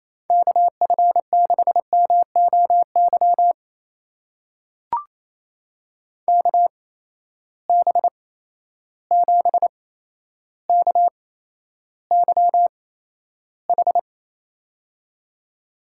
Morse code, heard as a keyed tone, about 27 wpm, spaced as for 12 wpm, 700 Hz. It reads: KF6MOY KB7KYH
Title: USA Call Signs - 6 Characters - 8x Character Spacing - 28wpm